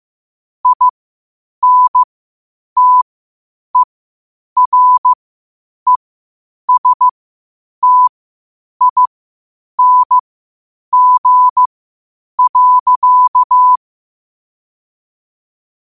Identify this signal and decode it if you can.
Morse code, heard as a keyed tone, about 15 wpm, spaced as for 8 wpm, 1000 Hz. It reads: INTERESTING.